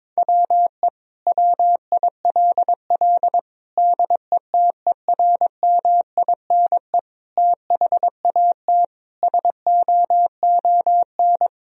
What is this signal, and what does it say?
Morse code, heard as a keyed tone, about 22 wpm, 700 Hz. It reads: WE WILL DETERMINE THAT SOON
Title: Sentences from Top 1000 Words - Review Part 4 - 22wpm